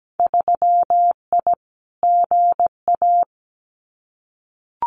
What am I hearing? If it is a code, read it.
Morse code, heard as a keyed tone, about 17 wpm, 700 Hz. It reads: 3I GA
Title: Winter Field Day Exchanges 17wpm